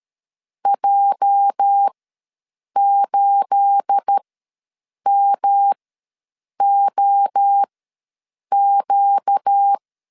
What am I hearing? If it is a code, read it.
Morse code, heard as a keyed tone, about 13 wpm, spaced as for 7 wpm, 800 Hz. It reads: J8MOQ
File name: LEZIONI / 9 28 ws 12wpm 3354